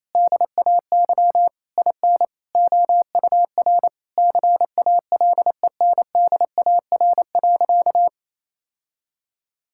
Morse code, heard as a keyed tone, about 28 wpm, 700 Hz. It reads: DAY IN OUR CALENDAR.
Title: Sentences from Top 1600 Words 28wpm